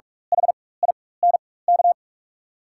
Morse code, heard as a keyed tone, about 44 wpm, spaced as for 21 wpm, 690 Hz. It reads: HINK
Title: 4x Character Spacing - Top 200 Words - 45wpm